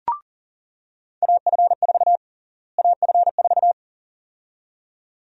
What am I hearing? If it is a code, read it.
Morse code, heard as a keyed tone, about 40 wpm, 700 Hz. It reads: AF4 AF4
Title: USA Call Sign Prefixes - Repeated 2x - 40wpm